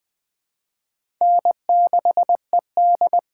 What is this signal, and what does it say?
Morse code, heard as a keyed tone, about 20 wpm, 700 Hz. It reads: N6ED
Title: USA Call Signs - 4 Characters - 4x Character Spacing - 20wpm